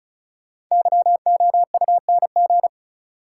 Morse code, heard as a keyed tone, about 34 wpm, 700 Hz. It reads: YOUNG
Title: Sentences from Top 400 Words - New Word Drill - 35wpm